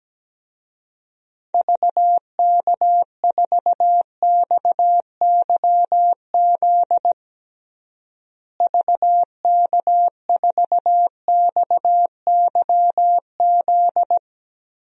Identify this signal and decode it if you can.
Morse code, heard as a keyed tone, about 17 wpm, 700 Hz. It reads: VK4XYZ VK4XYZ
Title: DXCC-100 Call Signs - 6 Characters - Repeated 3x - 17wpm